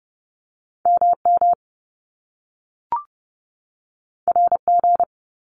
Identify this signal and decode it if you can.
Morse code, heard as a keyed tone, about 30 wpm, 700 Hz. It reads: MM RG
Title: Top Two Letter Combinations in Words 30wpm